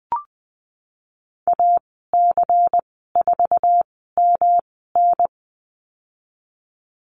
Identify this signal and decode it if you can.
Morse code, heard as a keyed tone, about 20 wpm, spaced as for 15 wpm, 700 Hz. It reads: AC4MN
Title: 154 Lesson 25g - Letter p - Call Signs 20wpm